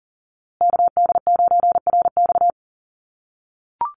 Morse code, heard as a keyed tone, about 40 wpm, 700 Hz. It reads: KD9RX
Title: USA Call Signs Repeated 3x 40wpm